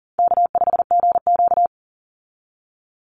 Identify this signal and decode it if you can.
Morse code, heard as a keyed tone, about 39 wpm, 700 Hz. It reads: K5GQ
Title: Call Signs Repeated 3x - 40wpm